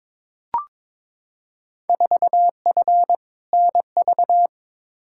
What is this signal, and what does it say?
Morse code, heard as a keyed tone, about 22 wpm, 700 Hz. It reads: 4F NV
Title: Field Day Exchanges Encore - 22wpm